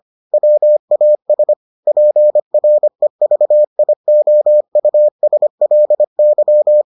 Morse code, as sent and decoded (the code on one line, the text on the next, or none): .-- .- ... / .--. .-. ...- .. --- ..- ... .-.. -.--
WAS PREVIOUSLY